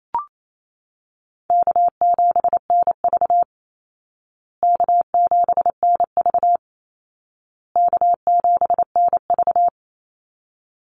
Morse code, heard as a keyed tone, about 28 wpm, 700 Hz. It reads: K7NV K7NV K7NV
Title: Easy Call Signs Repeated 3x - 28wpm